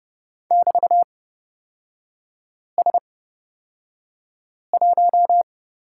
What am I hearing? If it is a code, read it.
Morse code, heard as a keyed tone, about 29 wpm, spaced as for 5 wpm, 700 Hz. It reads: <BT>S1
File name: Lesson 38e - Letter q - Three Characters 30wpm